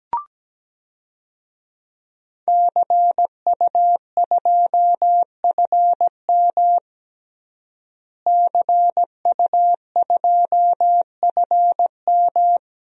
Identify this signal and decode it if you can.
Morse code, heard as a keyed tone, about 17 wpm, 700 Hz. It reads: CU2FM CU2FM
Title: DXCC-100 Call Signs - 5 Characters - Repeated 3x - 17wpm